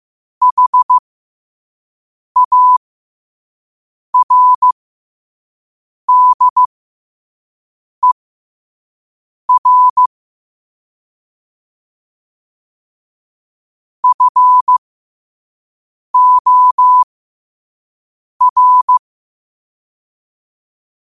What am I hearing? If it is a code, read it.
Morse code, heard as a keyed tone, about 15 wpm, spaced as for 5 wpm, 1000 Hz. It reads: HARDER FOR
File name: qotdfiles / 5wpm